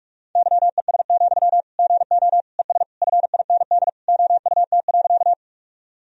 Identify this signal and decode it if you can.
Morse code, heard as a keyed tone, about 46 wpm, 700 Hz. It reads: YES, GO ES FIND OUT.
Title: Sentences from Top 500 Words and Abbreviations 45wpm